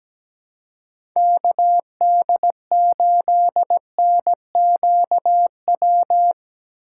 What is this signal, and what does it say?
Morse code, heard as a keyed tone, about 17 wpm, 700 Hz. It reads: KD8NQW
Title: USA Call Signs - 2x Character Spacing - 17wpm